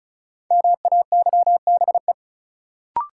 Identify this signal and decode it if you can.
Morse code, heard as a keyed tone, about 36 wpm, 700 Hz. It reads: MAYBE
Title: Sentences from Top 1000 Words - New Word Drill - 35wpm